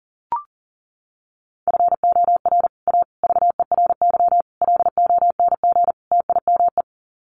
Morse code, heard as a keyed tone, about 40 wpm, 700 Hz. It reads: FOR A VERY LONG TIME
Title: Sets of 5 Words 40wpm